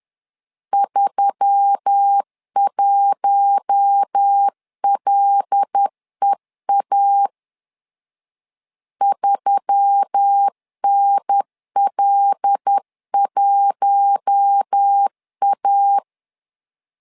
Morse code, heard as a keyed tone, about 10 wpm, 800 Hz. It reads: 31LEA 3NL1A
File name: LEZIONI / 5 LBVC / 5 L 10wpm 2259